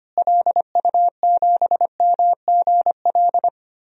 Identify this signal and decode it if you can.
Morse code, heard as a keyed tone, about 25 wpm, 700 Hz. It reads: LU7MGL
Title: DXCC-100 Call Signs - 6 Characters - Repeated 2x - 25wpm